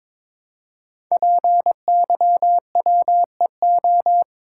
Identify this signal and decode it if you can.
Morse code, heard as a keyed tone, about 23 wpm, 700 Hz. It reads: PYWEO
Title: Code Groups - 5 Letters - 22wpm